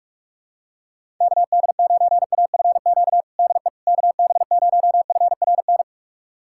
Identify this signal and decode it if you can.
Morse code, heard as a keyed tone, about 46 wpm, 700 Hz. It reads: KD9AFQ DE KB0FRN
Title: USA Callsign DE Callsign - 6 Characters - 45wpm